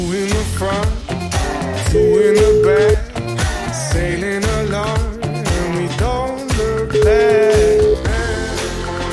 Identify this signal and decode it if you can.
Ringback tone (Europe, 425 Hz), with noise.